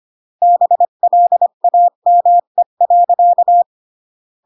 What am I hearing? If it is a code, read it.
Morse code, heard as a keyed tone, about 25 wpm, 700 Hz. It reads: BLAME.